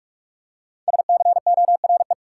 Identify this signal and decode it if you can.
Morse code, heard as a keyed tone, about 45 wpm, 700 Hz. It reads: SCORE
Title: Five Letter Words - 45wpm